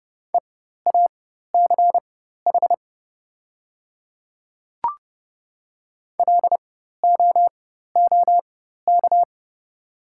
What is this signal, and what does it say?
Morse code, heard as a keyed tone, about 30 wpm, spaced as for 14 wpm, 700 Hz. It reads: EACH LOOK
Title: 4x Character Spacing - Top 100 Words - 30wpm